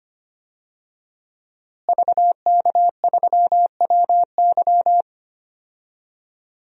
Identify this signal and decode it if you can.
Morse code, heard as a keyed tone, about 25 wpm, 700 Hz. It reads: VK3WY
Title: DXCC-100 Call Signs - 5 Characters - Repeated 2x - 25wpm